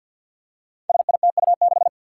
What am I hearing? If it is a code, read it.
Morse code, heard as a keyed tone, about 47 wpm, 700 Hz. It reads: SITU6
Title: Five Challenging Letter-Number Combinations 50wpm